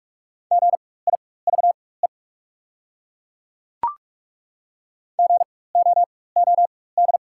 Morse code, heard as a keyed tone, about 43 wpm, spaced as for 21 wpm, 690 Hz. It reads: GIVE GOOD
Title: 4x Character Spacing - Top 300 Words - 45wpm